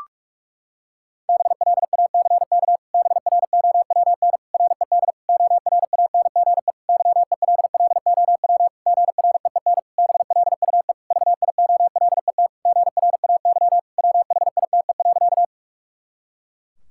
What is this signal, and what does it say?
Morse code, heard as a keyed tone, about 41 wpm, 700 Hz. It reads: BLACK BROWN RED ORANGE YELLOW GREEN BLUE VIOLET GRAY WHITE.
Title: Pangram Challenge v2 - 45wpm